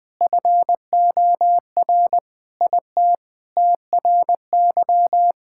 Morse code, heard as a keyed tone, about 20 wpm, 700 Hz. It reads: FOR IT TRY
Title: Sentences with 7-Word Phrases - Spoken Beginning - 20wpm